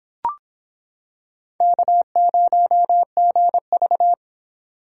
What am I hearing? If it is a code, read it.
Morse code, heard as a keyed tone, about 26 wpm, 700 Hz. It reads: K0GV